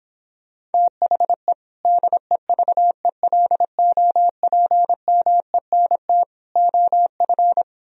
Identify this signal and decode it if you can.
Morse code, heard as a keyed tone, about 29 wpm, 700 Hz. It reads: THE DEVELOPMENT OF